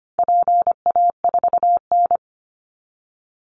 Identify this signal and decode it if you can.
Morse code, heard as a keyed tone, about 25 wpm, 700 Hz. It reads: PA4N
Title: Call Signs Repeated 2x - 25wpm